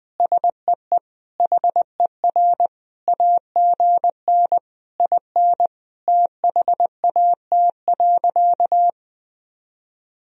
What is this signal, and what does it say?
Morse code, heard as a keyed tone, about 20 wpm, 700 Hz. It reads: SEE HER AGN IN THAT.
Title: Sentences from Top 200 Words and Abbreviations 20wpm